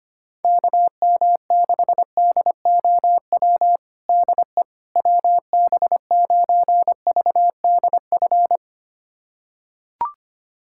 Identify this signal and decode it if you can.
Morse code, heard as a keyed tone, about 25 wpm, 700 Hz. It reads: KM6DOW DE WB9VDF
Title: USA Callsign DE Callsign - 6 Characters - 25wpm